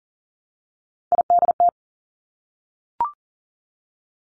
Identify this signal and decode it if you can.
Morse code, heard as a keyed tone, about 40 wpm, 700 Hz. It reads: IDT
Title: Three Challenging Letter-Number Combinations 40wpm